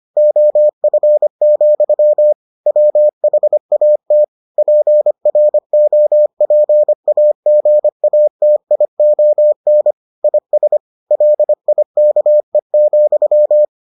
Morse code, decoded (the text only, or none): OF, WHAT PROPAGATION IS LIKE,